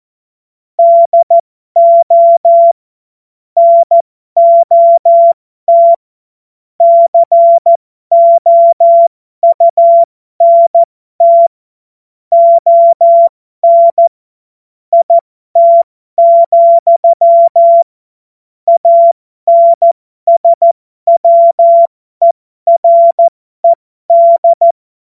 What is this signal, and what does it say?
Morse code, heard as a keyed tone, about 14 wpm, spaced as for 12 wpm, 680 Hz. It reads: DO NOT COUNT ON IT, ANSWERED